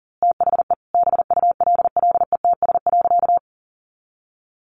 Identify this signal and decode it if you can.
Morse code, heard as a keyed tone, about 40 wpm, 700 Hz. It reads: THE BULLETS.
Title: A Sound Of Thunder 40wpm